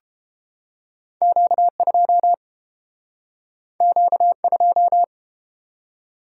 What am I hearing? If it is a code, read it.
Morse code, heard as a keyed tone, about 29 wpm, 700 Hz. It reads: Q2 Q2